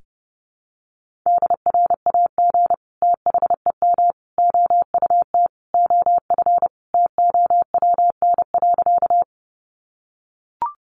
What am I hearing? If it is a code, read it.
Morse code, heard as a keyed tone, about 30 wpm, 700 Hz. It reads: DRAG THEM OUT OF TOWN.
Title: Sentences from Top 1500 Words 30wpm